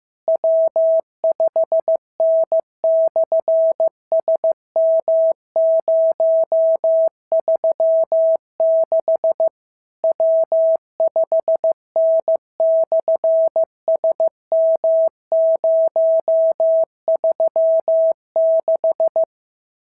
Morse code, heard as a keyed tone, about 15 wpm, 650 Hz. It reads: W5N/SM036 W5N/SM036
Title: SOTA QSO Practice 15wpm